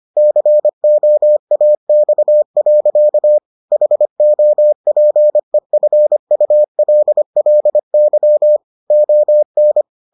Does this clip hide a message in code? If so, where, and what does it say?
Yes: Morse code from 0.2 to 9.8 s, heard as a keyed tone, about 25 wpm, 600 Hz. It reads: COAX. HOPEFULLY ON